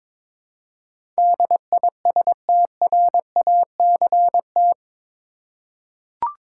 Morse code, heard as a keyed tone, about 22 wpm, 700 Hz. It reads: DISTRACT
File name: practice / 22wpm / Eight Letter Words - 22wpm